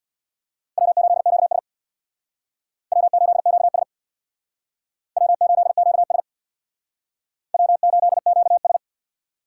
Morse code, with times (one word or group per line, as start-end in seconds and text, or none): 0.8-1.6 W8QS
2.9-3.8 W8QS
5.2-6.2 W8QS
7.5-8.8 W8QS